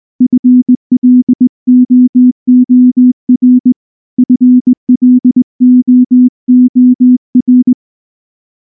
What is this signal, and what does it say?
Morse code, heard as a keyed tone, about 19 wpm, 250 Hz. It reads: FLOOR FLOOR
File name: morse / morse-training-20WPM-2